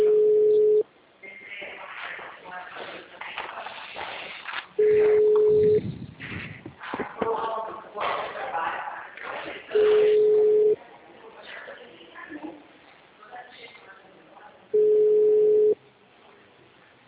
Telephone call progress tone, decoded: ringback tone

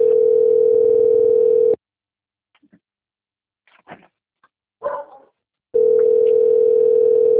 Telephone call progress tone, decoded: ringback tone

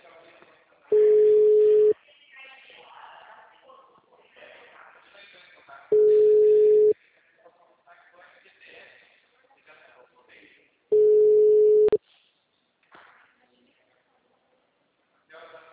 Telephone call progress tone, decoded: ringback tone

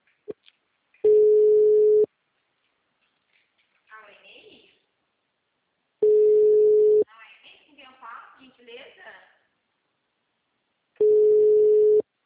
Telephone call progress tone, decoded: ringback tone

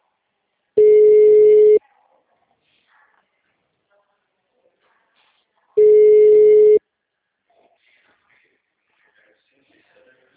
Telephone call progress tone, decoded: ringback tone

